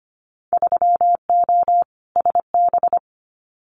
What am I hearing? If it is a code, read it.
Morse code, heard as a keyed tone, about 25 wpm, 700 Hz. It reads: 3O SB
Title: Winter Field Day Exchanges Encore 25wpm